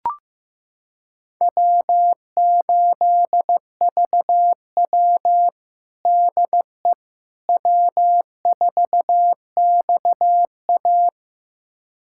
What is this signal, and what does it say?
Morse code, heard as a keyed tone, about 15 wpm, 700 Hz. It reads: W8VW DE W4XA
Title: USA Callsign DE Callsign - 4 Characters - 15wpm